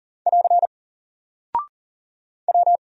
Morse code, heard as a keyed tone, about 40 wpm, 700 Hz. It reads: <AR> W
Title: Single Character (Letter:Number:Punctuation:Prosign) 40wpm